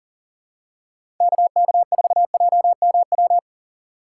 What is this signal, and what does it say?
Morse code, heard as a keyed tone, about 40 wpm, 700 Hz. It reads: KK4JMW
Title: USA Call Signs - 6 Characters - 3x Character Spacing - 40wpm